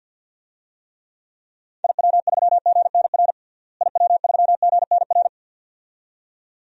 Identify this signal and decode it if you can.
Morse code, heard as a keyed tone, about 47 wpm, 700 Hz. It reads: IW3GNR IW3GNR
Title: DXCC-100 Call Signs - 6 Characters - Repeated 2x - 50wpm